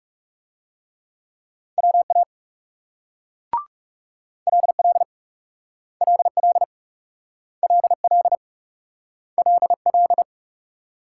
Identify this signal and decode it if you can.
Morse code, heard as a keyed tone, about 38 wpm, 700 Hz. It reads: WA LL LL LL LL